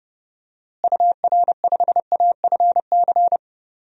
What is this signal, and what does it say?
Morse code, heard as a keyed tone, about 30 wpm, 700 Hz. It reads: UR5AFC
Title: DXCC-100 Call Signs - 6 Characters - Repeated 2x - 30wpm